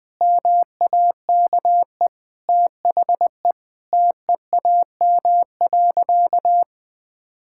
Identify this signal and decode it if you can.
Morse code, heard as a keyed tone, about 21 wpm, 700 Hz. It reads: MAKE THE TEAM.
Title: Sentences with 3-Word Phrases - 20wpm